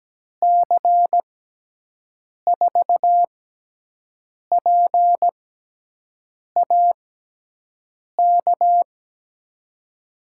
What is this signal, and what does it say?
Morse code, heard as a keyed tone, about 17 wpm, spaced as for 6 wpm, 700 Hz. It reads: C4PAK
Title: USA Call Signs - 6 Characters - 6x Character Spacing - 17wpm